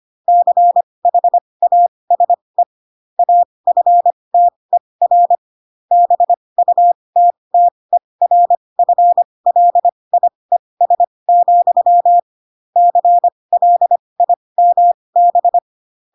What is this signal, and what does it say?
Morse code, heard as a keyed tone, about 25 wpm, spaced as for 20 wpm, 700 Hz. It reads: CHASE AFTER BUTTERFLIES, CLIMB
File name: Pinocchio / Pinocchio ch4 15 25 ik1hge